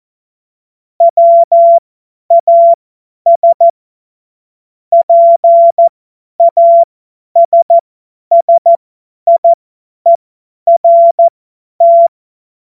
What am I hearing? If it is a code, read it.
Morse code, heard as a keyed tone, about 14 wpm, spaced as for 10 wpm, 680 Hz. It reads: WAS PASSIERT
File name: Morsekurs / Witze / Witz 27. April 2021, 50 F70